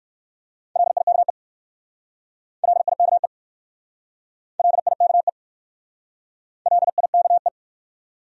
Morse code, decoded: LIKE LIKE LIKE LIKE

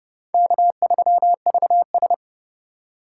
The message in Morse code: K3VS

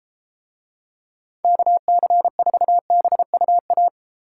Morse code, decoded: KC4BUA

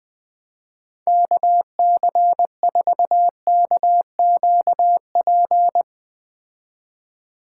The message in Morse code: KC4KQP

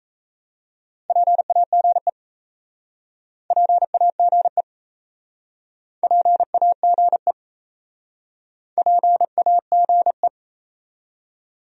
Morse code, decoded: PAGE PAGE PAGE PAGE

